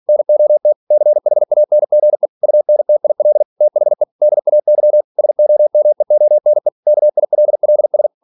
Morse code, heard as a keyed tone, about 46 wpm, 600 Hz. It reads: NOT CHANGE UNTIL THE DAY SOMEONE KILLS
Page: Never 47wpm